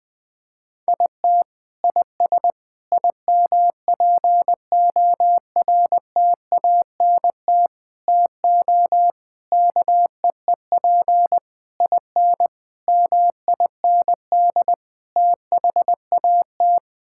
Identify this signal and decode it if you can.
Morse code, heard as a keyed tone, about 20 wpm, 700 Hz. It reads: IT IS IMPORTANT TO KEEP IN MIND THAT
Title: Sets of 8 Words - 20wpm